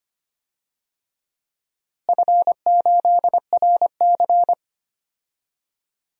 Morse code, heard as a keyed tone, about 25 wpm, 700 Hz. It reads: F8RC
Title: DXCC-100 Call Signs - 4 Characters - Repeated 2x - 25wpm